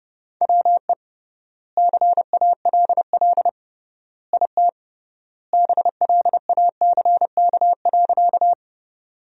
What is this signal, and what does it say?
Morse code, heard as a keyed tone, about 30 wpm, 700 Hz. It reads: WE CALL IT BLACK.
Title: 3x Word-Spacing - Sentences from Top 400 Words - 30wpm